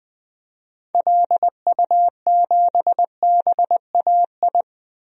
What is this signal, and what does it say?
Morse code, heard as a keyed tone, about 21 wpm, 700 Hz. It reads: LU7BAI
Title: DXCC-100 Call Signs - 6 Characters - Repeated 2x - 20wpm